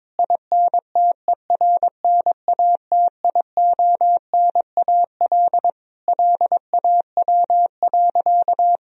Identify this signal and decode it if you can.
Morse code, heard as a keyed tone, about 21 wpm, 700 Hz. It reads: INTERNATIONAL LAW.
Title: Sentences from Top 1000 Words - Review Part 4 - 22wpm